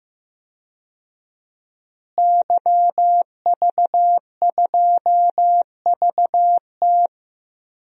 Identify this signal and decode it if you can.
Morse code, heard as a keyed tone, about 15 wpm, 700 Hz. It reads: YV2VT
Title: DXCC-100 Call Signs - 5 Characters - Repeated 3x - 15wpm